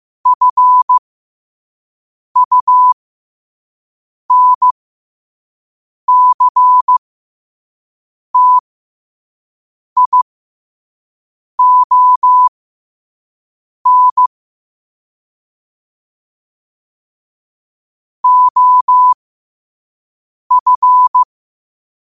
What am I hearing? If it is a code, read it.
Morse code, heard as a keyed tone, about 15 wpm, spaced as for 5 wpm, 1000 Hz. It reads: FUNCTION OF
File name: qotdfiles / 5wpm